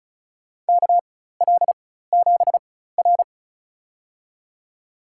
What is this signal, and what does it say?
Morse code, heard as a keyed tone, about 35 wpm, spaced as for 16 wpm, 700 Hz. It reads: KL7R
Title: USA Call Signs - 4 Characters - 4x Character Spacing - 35wpm